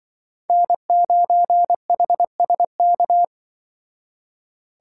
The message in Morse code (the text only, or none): N9HSK